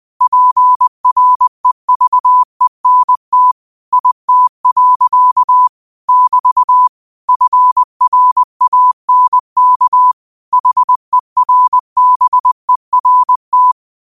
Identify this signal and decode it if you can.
Morse code, heard as a keyed tone, about 20 wpm, 1000 Hz. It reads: PREVENT IT. <BT> FRANK HERBERT
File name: qotdfiles / 20wpm